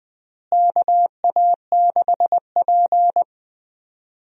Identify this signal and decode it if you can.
Morse code, heard as a keyed tone, about 20 wpm, 700 Hz. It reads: KA6P
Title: USA Call Signs - 4 Characters - Repeated 2x - 20wpm